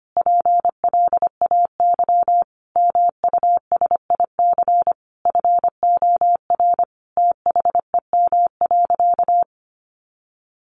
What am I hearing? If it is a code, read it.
Morse code, heard as a keyed tone, about 25 wpm, 700 Hz. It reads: PLAY MUSIC FOR THEM.
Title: Sentences from Top 300 Words 25wpm